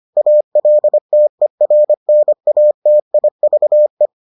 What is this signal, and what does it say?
Morse code, heard as a keyed tone, about 25 wpm, 600 Hz. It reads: ALTERNATIVE